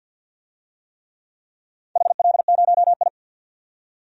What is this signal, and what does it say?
Morse code, heard as a keyed tone, about 52 wpm, 700 Hz. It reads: HL0I
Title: DXCC-100 Call Signs - 4 Characters - 50wpm